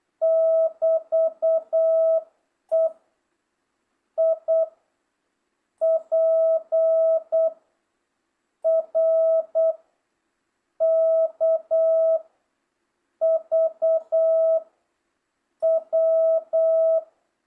Morse code, heard as a keyed tone, about 8 wpm, 650 Hz. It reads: <BT> E I P R K V W